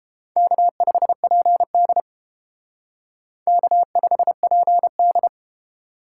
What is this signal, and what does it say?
Morse code, heard as a keyed tone, about 30 wpm, 700 Hz. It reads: K5PD K5PD